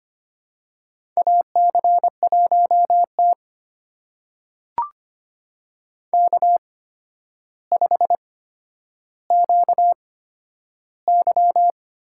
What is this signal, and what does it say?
Morse code, heard as a keyed tone, about 25 wpm, spaced as for 11 wpm, 700 Hz. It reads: AC1T K5QY
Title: USA Call Signs - 4 Characters - 8x Character Spacing - 25wpm